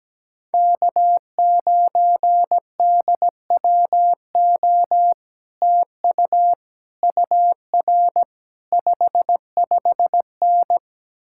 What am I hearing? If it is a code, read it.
Morse code, heard as a keyed tone, about 17 wpm, 700 Hz. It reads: K9DWO TU UR 55N
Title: Speed-Racing Common QSO Phrases Encore - 17wpm